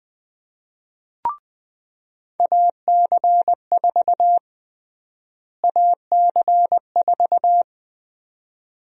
Morse code, heard as a keyed tone, about 20 wpm, 700 Hz. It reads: AC4 AC4
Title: USA Call Sign Prefixes - Repeated 2x - 20wpm